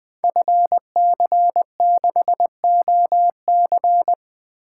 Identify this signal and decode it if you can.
Morse code, heard as a keyed tone, about 20 wpm, 700 Hz. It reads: FC6OC
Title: Code Groups - 5 Letter:Numbers - 20wpm